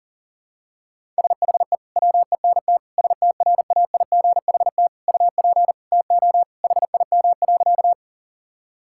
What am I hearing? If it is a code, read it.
Morse code, heard as a keyed tone, about 40 wpm, 700 Hz. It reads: SHE WENT STRAIGHT UP TO HIM.
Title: Sentences from Top 1000 Words - Review Part 6 - 40wpm